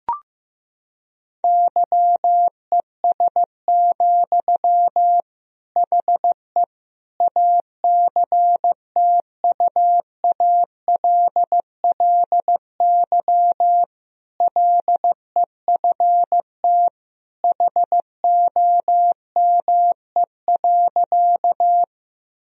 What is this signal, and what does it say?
Morse code, heard as a keyed tone, about 16 wpm, 700 Hz. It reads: YES, HE ACTUALLY LEFT HOME.
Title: Sentences from Top 1000 Words - Review Part 4 - 15wpm